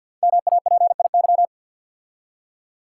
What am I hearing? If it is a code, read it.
Morse code, heard as a keyed tone, about 48 wpm, 700 Hz. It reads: MAPIY
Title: Code Groups - 5 Letters - 50wpm